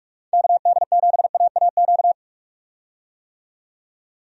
Morse code, heard as a keyed tone, about 46 wpm, 700 Hz. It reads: KD7AAQ
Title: USA Call Signs - 6 Characters - 45wpm